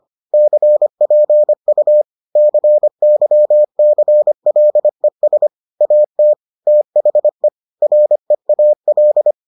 Morse code, heard as a keyed tone, about 25 wpm, 600 Hz. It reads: CPU CYCLES AT THE REAL